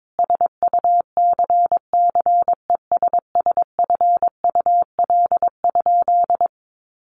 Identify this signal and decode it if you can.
Morse code, heard as a keyed tone, about 22 wpm, 700 Hz. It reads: SUCCESSFUL?
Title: A Sound Of Thunder 22wpm